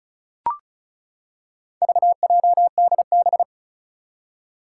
Morse code, heard as a keyed tone, about 36 wpm, 700 Hz. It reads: VJDB
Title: Four Challenging Letter-Number Combinations 35wpm